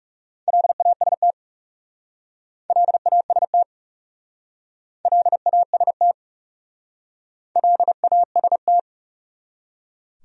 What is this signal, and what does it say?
Morse code, heard as a keyed tone, about 37 wpm, 700 Hz. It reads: LAST LAST LAST LAST
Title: Speed-Racing Top 500 Words 30wpm